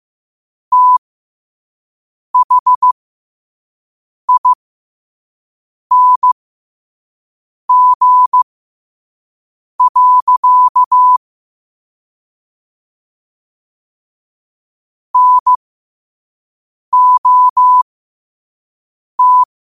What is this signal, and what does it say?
Morse code, heard as a keyed tone, about 15 wpm, spaced as for 5 wpm, 1000 Hz. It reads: THING. NOT
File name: qotdfiles / 5wpm